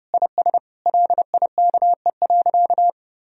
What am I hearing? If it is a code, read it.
Morse code, heard as a keyed tone, about 30 wpm, 700 Hz. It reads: IS LIKE.
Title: Sentences from Top 1000 Words - Review Part 3 - 30wpm